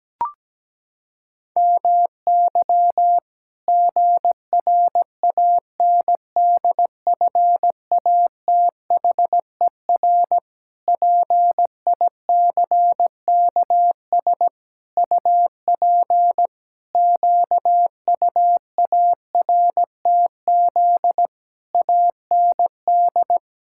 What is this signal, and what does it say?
Morse code, heard as a keyed tone, about 17 wpm, 700 Hz. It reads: MY GRANDFATHER PICKS UP QUARTZ AND
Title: Pangram Challenge v2 - 17wpm